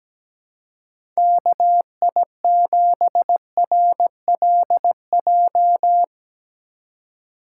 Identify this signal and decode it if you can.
Morse code, heard as a keyed tone, about 17 wpm, 700 Hz. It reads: KI7RLJ
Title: Speed Racing - USA 2x3 Call Signs 17wpm